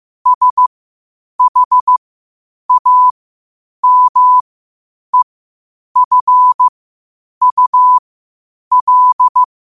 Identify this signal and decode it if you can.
Morse code, heard as a keyed tone, about 15 wpm, spaced as for 8 wpm, 1000 Hz. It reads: SHAMEFUL
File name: qotdfiles / 8wpm